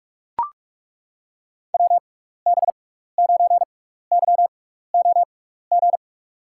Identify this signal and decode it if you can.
Morse code, heard as a keyed tone, about 42 wpm, spaced as for 15 wpm, 700 Hz. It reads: WB9YOG
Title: USA Call Signs - 6 Characters - 6x Character Spacing - 45wpm